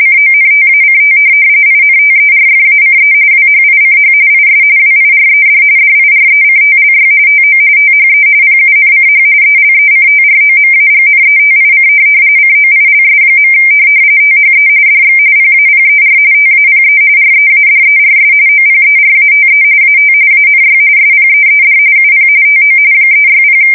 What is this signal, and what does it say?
RTTY, about 45.45 baud. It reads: CONTACT INTERFACE WHICH IS WIRED DIRECTLY TO THE PUNCH ELECTROMAGNETIC COILS. THE DX PAPER TAPE READERS OPERATE AT ANY SPEED UP TO 360
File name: fsk170 / teletype-corp